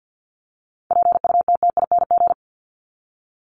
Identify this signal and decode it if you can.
Morse code, heard as a keyed tone, about 50 wpm, 700 Hz. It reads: PUTTING
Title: ING Suffix - Top 100 Words with Base Words 50wpm